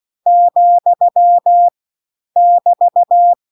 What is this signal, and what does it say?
Morse code, heard as a keyed tone, about 16 wpm, 700 Hz. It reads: , <BT>